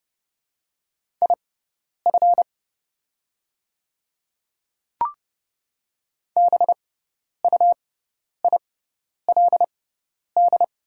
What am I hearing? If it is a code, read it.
Morse code, heard as a keyed tone, about 32 wpm, spaced as for 10 wpm, 700 Hz. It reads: IF BUILD
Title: 6x Character Spacing - Top 200 Words - 30wpm